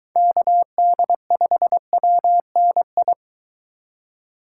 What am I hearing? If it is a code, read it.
Morse code, heard as a keyed tone, about 22 wpm, 700 Hz. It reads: KD5WNI